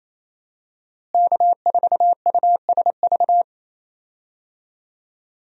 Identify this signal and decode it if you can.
Morse code, heard as a keyed tone, about 28 wpm, 700 Hz. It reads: K4USV
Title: USA Call Signs - 5 Characters - Repeated 2x - 28wpm